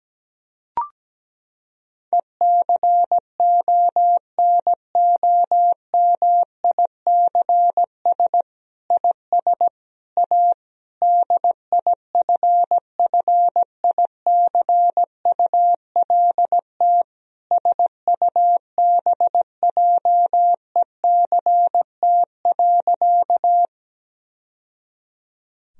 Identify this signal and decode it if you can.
Morse code, heard as a keyed tone, about 17 wpm, 700 Hz. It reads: ECONOMICS IS A DIFFICULT SUBJECT.
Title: Sentences from Top 1000 Words - Review Part 3 - 17wpm